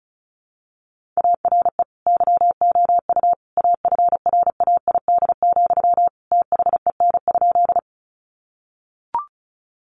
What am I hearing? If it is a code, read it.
Morse code, heard as a keyed tone, about 34 wpm, 700 Hz. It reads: ARE YOU AFRAID, THEN?
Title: Sentences from Top 1500 Words 35wpm